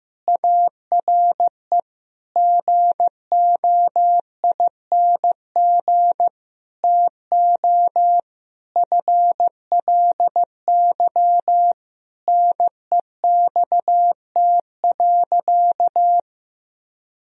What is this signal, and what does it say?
Morse code, heard as a keyed tone, about 15 wpm, 700 Hz. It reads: ARE GOING TO FLY NEXT.
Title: Sentences from Top 1000 Words - Review Part 2 - 15wpm